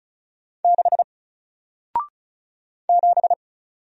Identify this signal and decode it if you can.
Morse code, heard as a keyed tone, about 36 wpm, 700 Hz. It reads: 6 7
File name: Single-Letter-Number-35wpm